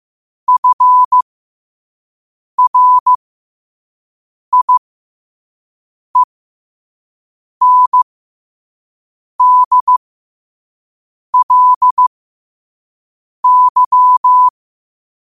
Morse code, heard as a keyed tone, about 15 wpm, spaced as for 5 wpm, 1000 Hz. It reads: FRIENDLY
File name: qotdfiles / 5wpm